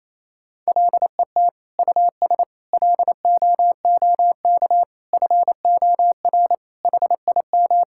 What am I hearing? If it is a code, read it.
Morse code, heard as a keyed tone, about 28 wpm, 700 Hz. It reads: LET US LOOK FOR HIM
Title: Sentences from Top 1600 Words 28wpm